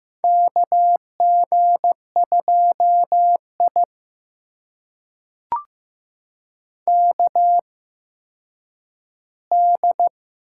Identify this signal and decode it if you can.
Morse code, heard as a keyed tone, about 15 wpm, spaced as for 6 wpm, 700 Hz. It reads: KG2IKD
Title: USA Call Signs - 8x Character Spacing - 15wpm